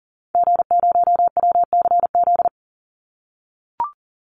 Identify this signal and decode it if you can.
Morse code, heard as a keyed tone, about 40 wpm, 700 Hz. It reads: G0WCZ
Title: Call Signs - 40wpm